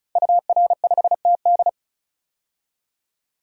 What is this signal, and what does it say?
Morse code, heard as a keyed tone, about 34 wpm, 700 Hz. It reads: UR5TD
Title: DXCC-100 Call Signs - 5 Characters - 35wpm